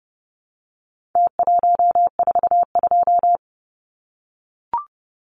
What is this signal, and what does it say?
Morse code, heard as a keyed tone, about 30 wpm, 700 Hz. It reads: T142
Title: Numbers - Contest Style - 4-Digit 30wpm